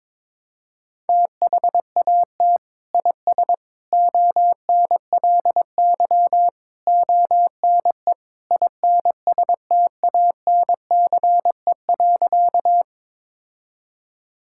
Morse code, heard as a keyed tone, about 22 wpm, 700 Hz. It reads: THAT IS ONLY ONE INSTANCE.
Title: Sentences from Top 1000 Words - Review Part 5 - 22wpm